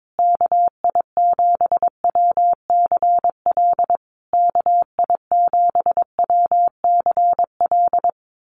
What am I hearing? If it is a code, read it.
Morse code, heard as a keyed tone, about 22 wpm, 700 Hz. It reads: KI7WCL KI7WCL
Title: USA Call Signs Repeated 3x 22wpm